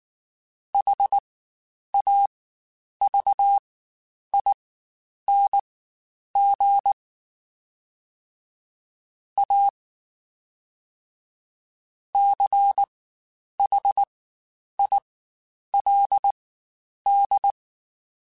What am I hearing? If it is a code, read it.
Morse code, heard as a keyed tone, about 19 wpm, spaced as for 9 wpm, 800 Hz. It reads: HAVING A CHILD
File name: Quotes Spaced / Quotes Spaced-mp3s / ChildAsleep sp @18